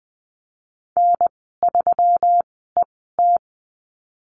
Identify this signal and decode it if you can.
Morse code, heard as a keyed tone, about 20 wpm, spaced as for 15 wpm, 700 Hz. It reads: N3ET
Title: 147 Lesson 24g - Letter y - Call Signs 20wpm